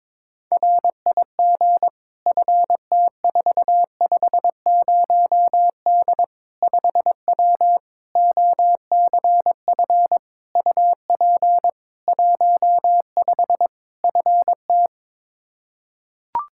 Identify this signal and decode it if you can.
Morse code, heard as a keyed tone, about 23 wpm, 700 Hz. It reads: RIG FT450D 5W OCF UP 15 FT
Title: K1USN SST QSO Practice 22wpm